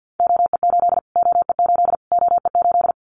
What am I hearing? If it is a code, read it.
Morse code, heard as a keyed tone, about 50 wpm, 700 Hz. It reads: OE8 OE8 OE8
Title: Callsign Prefixes for Top 25 DXCC - Repeated 3x - 50wpm